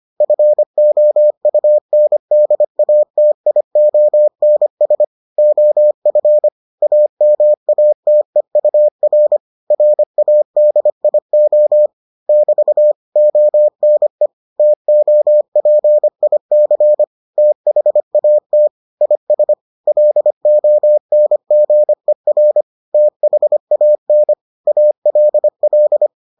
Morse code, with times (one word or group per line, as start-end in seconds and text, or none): 0.2-5.0 FOUNDATIONS
5.4-6.5 OF
6.8-9.4 AMATEUR
9.7-11.9 RADIO
12.3-12.9 <BT>
13.2-14.3 ONE
14.6-17.0 TOPIC
17.4-18.7 THAT
19.0-19.5 IS
19.9-22.6 LONGER
22.9-24.3 THAN
24.7-26.1 ALL